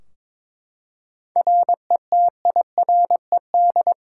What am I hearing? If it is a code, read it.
Morse code, heard as a keyed tone, about 22 wpm, 700 Hz. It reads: RETIRED
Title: Practice QSO - Exchange 3 - More Info - 22wpm